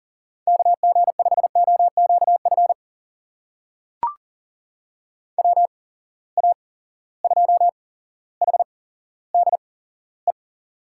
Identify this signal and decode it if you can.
Morse code, heard as a keyed tone, about 40 wpm, spaced as for 17 wpm, 700 Hz. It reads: KG5OQF WA2HDE
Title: USA Call Signs - 6 Characters - 8x Character Spacing - 40wpm